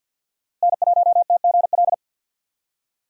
Morse code, heard as a keyed tone, about 46 wpm, 700 Hz. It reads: N1TGL
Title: USA Call Signs - 5 Characters - Repeated 2x - 50wpm